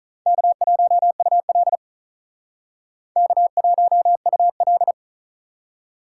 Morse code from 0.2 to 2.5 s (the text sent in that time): K1UL